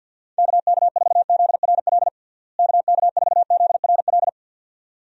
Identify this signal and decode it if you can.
Morse code, heard as a keyed tone, about 51 wpm, 700 Hz. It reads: KK4ZRL KK4ZRL
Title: USA Call Signs - 6 Characters - Repeated 2x - 50wpm